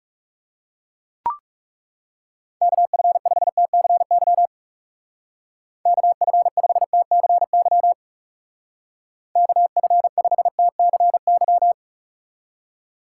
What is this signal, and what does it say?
Morse code, heard as a keyed tone, about 40 wpm, 700 Hz. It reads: KF5TCY KF5TCY KF5TCY